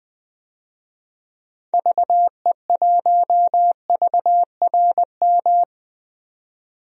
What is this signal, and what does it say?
Morse code, heard as a keyed tone, about 21 wpm, 700 Hz. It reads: VE1VRM
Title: DXCC-100 Call Signs - 6 Characters - 20wpm